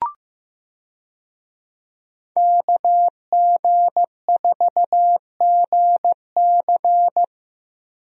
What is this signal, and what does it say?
Morse code, heard as a keyed tone, about 15 wpm, 700 Hz. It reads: KG4GC
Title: USA Call Signs - 5 Characters - Repeated 3x - 15wpm